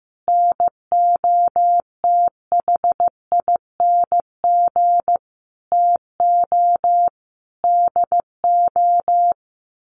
Morse code, heard as a keyed tone, about 15 wpm, 700 Hz. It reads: NOTHING TO DO
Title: Sets of 5 Words Encore 15wpm